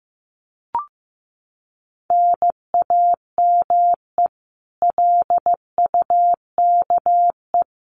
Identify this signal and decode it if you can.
Morse code, heard as a keyed tone, about 15 wpm, 700 Hz. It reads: NAME LUKE
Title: Common QSO Phrases 15wpm